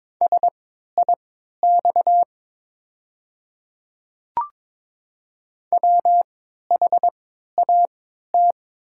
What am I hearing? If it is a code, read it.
Morse code, heard as a keyed tone, about 22 wpm, spaced as for 12 wpm, 690 Hz. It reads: SIX WHAT
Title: 3x Character Spacing - Top 500 Words - 22wpm